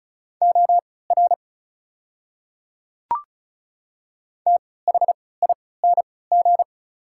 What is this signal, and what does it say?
Morse code, heard as a keyed tone, about 35 wpm, spaced as for 20 wpm, 700 Hz. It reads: OR THING
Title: 3x Character Spacing - Top 100 Words - 35wpm